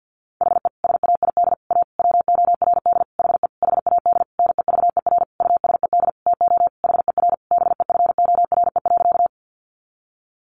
Morse code, heard as a keyed tone, about 50 wpm, 700 Hz. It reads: HE SAID A WORD HE HAD NEVER USED TO HER BEFORE.
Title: Sentences from Top 600 Words 50wpm